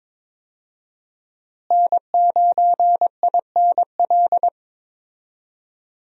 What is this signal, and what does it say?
Morse code, heard as a keyed tone, about 22 wpm, 700 Hz. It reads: N9INL
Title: USA Call Signs - 5 Characters - Repeated 3x - 22wpm